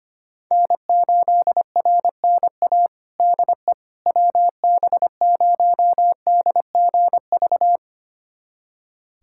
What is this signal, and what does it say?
Morse code, heard as a keyed tone, about 25 wpm, 700 Hz. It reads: N8RNA DE WB0DGV
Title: USA Callsign DE Callsign 25wpm